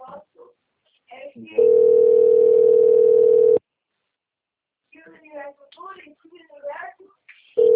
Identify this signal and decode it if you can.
Ringback tone (North America), 440 + 480 Hz, with speech.
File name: rg-1600-unknown- - - .61133